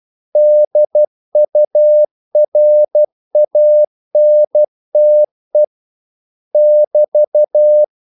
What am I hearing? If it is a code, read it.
Morse code, heard as a keyed tone, about 12 wpm, 600 Hz. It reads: DURANTE <BT>
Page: English quotations in CW (telegraphy - morse)